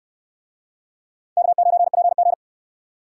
Morse code, heard as a keyed tone, about 68 wpm, 700 Hz. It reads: K9PG